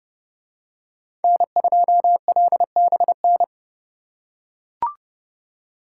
Morse code, heard as a keyed tone, about 28 wpm, 700 Hz. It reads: N2LBN